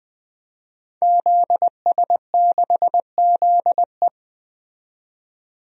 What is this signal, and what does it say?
Morse code, heard as a keyed tone, about 20 wpm, 700 Hz. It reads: ZS6ZE
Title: DXCC-100 Call Signs - 5 Characters - 20wpm